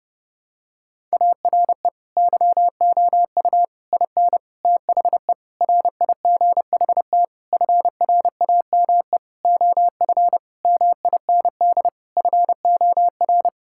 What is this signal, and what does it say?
Morse code, heard as a keyed tone, about 32 wpm, 700 Hz. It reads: ARE YOU IN THE RIGHT FRAME OF MIND FOR
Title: Sentences from Top 1000 Words - Review Part 6 - 30wpm